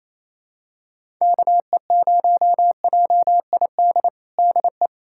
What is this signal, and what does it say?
Morse code, heard as a keyed tone, about 28 wpm, 700 Hz. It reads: KE0JID DE
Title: USA Callsign DE Callsign - 6 Characters - 28wpm